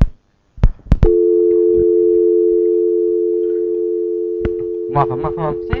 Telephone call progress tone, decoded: dial tone